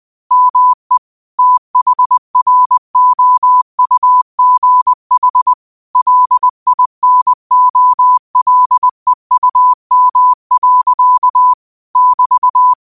Morse code, heard as a keyed tone, about 20 wpm, 1000 Hz. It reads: ME THROUGH LINOLEUM. <BT>
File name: qotdfiles / 20wpm